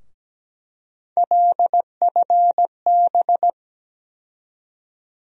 Morse code, heard as a keyed tone, about 17 wpm, 700 Hz. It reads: LFB
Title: USA Call Sign Postfixes - 3 Letters 17wpm